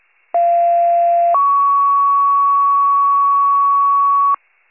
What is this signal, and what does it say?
Two-tone page, heard as a one-second tone, then a three-second tone, about 690 then 1120 Hz, with noise. It is heard from 0.3 to 4.4 s.